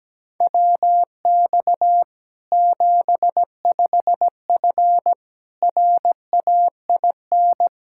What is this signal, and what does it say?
Morse code, heard as a keyed tone, about 17 wpm, 700 Hz. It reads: WX 75F RAIN
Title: Practice QSO - Exchange 2 - Info - 17wpm